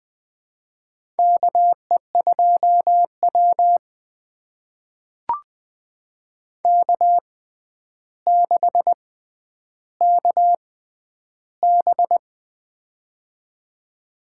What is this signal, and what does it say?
Morse code, heard as a keyed tone, about 20 wpm, spaced as for 10 wpm, 700 Hz. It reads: KE2W K6KB
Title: USA Call Signs - 4 Characters - 6x Character Spacing - 20wpm